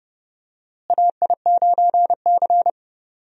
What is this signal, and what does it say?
Morse code, heard as a keyed tone, about 30 wpm, 700 Hz. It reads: AI9C
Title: USA Call Signs - 4 Characters - Repeated 3x - 30wpm